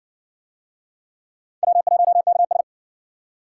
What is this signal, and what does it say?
Morse code, heard as a keyed tone, about 58 wpm, 700 Hz. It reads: W1KS